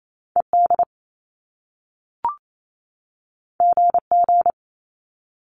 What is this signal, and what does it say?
Morse code, heard as a keyed tone, about 28 wpm, 700 Hz. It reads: ED GG
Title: Top Two Letter Combinations in Words 28wpm